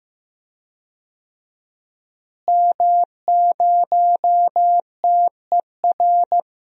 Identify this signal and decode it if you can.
Morse code, heard as a keyed tone, about 15 wpm, 700 Hz. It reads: M0TER
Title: DXCC-100 Call Signs - 5 Characters - Repeated 2x - 15wpm